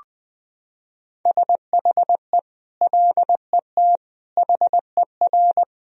Morse code, heard as a keyed tone, about 20 wpm, 700 Hz. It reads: SHE LET HER
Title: Sentences from Top 1000 Words - Review Part 1 - 20wpm